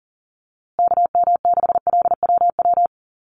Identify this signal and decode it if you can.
Morse code, heard as a keyed tone, about 40 wpm, 700 Hz. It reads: KM6LWW
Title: USA Call Signs Repeated 2x 40wpm